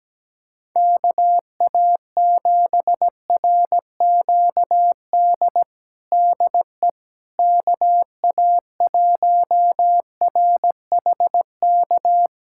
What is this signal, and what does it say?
Morse code, heard as a keyed tone, about 17 wpm, 700 Hz. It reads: KA7RQD DE KA1RHK
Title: USA Callsign DE Callsign - 6 Characters - 17wpm